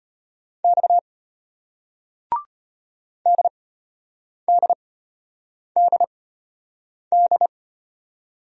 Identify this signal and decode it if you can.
Morse code, heard as a keyed tone, about 33 wpm, spaced as for 8 wpm, 700 Hz. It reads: X DDDD